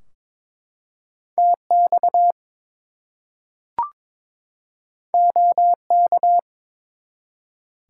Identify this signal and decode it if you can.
Morse code, heard as a keyed tone, about 22 wpm, 700 Hz. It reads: TX OK
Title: US States and Canadian Provinces 22wpm